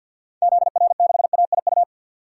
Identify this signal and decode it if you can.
Morse code, heard as a keyed tone, about 51 wpm, 700 Hz. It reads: ZR6AIU
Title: DXCC-100 Call Signs - 6 Characters - 50wpm